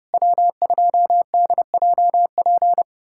Morse code, heard as a keyed tone, about 30 wpm, 700 Hz. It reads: W2DJP
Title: USA Callsign DE Callsign - 5 Characters - 30wpm